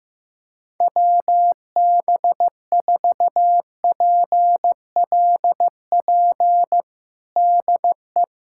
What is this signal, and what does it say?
Morse code, heard as a keyed tone, about 15 wpm, 700 Hz. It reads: WB4PLP DE
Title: USA Callsign DE Callsign 15wpm